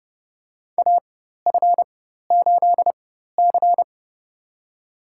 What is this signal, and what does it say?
Morse code, heard as a keyed tone, about 30 wpm, spaced as for 14 wpm, 700 Hz. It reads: AF8C
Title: USA Call Signs - 4 Characters - 4x Character Spacing - 30wpm